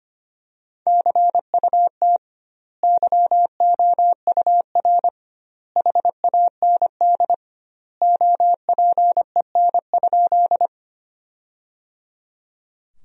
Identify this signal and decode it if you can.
Morse code, heard as a keyed tone, about 25 wpm, 700 Hz. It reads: CUT YOUR HAND OPEN?
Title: 2x Word-Spacing - Sentences from Top 300 Words - 25wpm